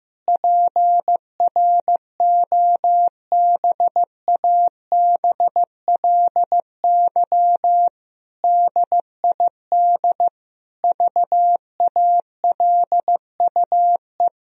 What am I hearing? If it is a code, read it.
Morse code, heard as a keyed tone, about 15 wpm, 700 Hz. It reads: PROBABLY DID VALUE
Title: Sentences from Top 1000 Words - Review Part 4 - 15wpm